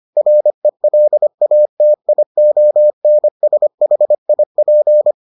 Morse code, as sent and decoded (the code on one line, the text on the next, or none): .-. .-.. .- - .. --- -. ... .... .. .--.
RELATIONSHIP